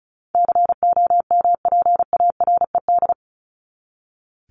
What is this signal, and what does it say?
Morse code, heard as a keyed tone, about 34 wpm, 700 Hz. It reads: COMPARED
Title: ED Suffix - Top 100 Words with Base Words 35wpm